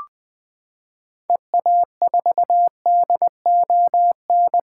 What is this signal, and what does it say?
Morse code, heard as a keyed tone, about 21 wpm, 700 Hz. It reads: EA4DON
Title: 2023 SOTA Activator Call Signs - 20wpm